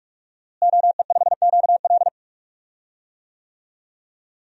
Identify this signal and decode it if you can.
Morse code, heard as a keyed tone, about 45 wpm, 700 Hz. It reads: OE5QL
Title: DXCC-100 Call Signs - 5 Characters - Repeated 2x - 45wpm